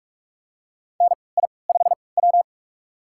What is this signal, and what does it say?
Morse code, heard as a keyed tone, about 44 wpm, spaced as for 25 wpm, 700 Hz. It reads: NI5W
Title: USA Call Signs - 4 Characters - 3x Character Spacing - 45wpm